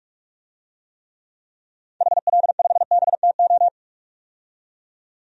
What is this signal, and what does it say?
Morse code, heard as a keyed tone, about 41 wpm, 700 Hz. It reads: HL5BTO
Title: DXCC-100 Call Signs - 6 Characters - 45wpm